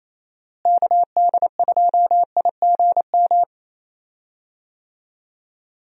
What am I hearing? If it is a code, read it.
Morse code, heard as a keyed tone, about 28 wpm, 700 Hz. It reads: KD2IGM
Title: USA Call Signs - 6 Characters - Repeated 2x - 28wpm